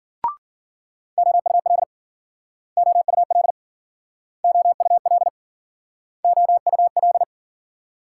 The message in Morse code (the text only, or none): OUL OUL OUL OUL